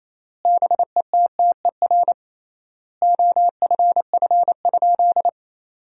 Morse code, heard as a keyed tone, about 28 wpm, 700 Hz. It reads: BETTER OFF?
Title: 3x Word-Spacing - Sentences from Top 500 Words - 28wpm